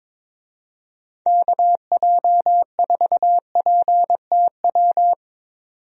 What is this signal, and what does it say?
Morse code, heard as a keyed tone, about 21 wpm, 700 Hz. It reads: KJ4PTW
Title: USA Call Signs - 6 Characters - 22wpm